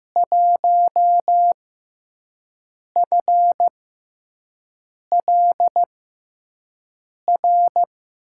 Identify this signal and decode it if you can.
Morse code, heard as a keyed tone, about 15 wpm, spaced as for 5 wpm, 700 Hz. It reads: 1FLR
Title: USA Call Signs - 5 Characters - 6x Character Spacing - 15wpm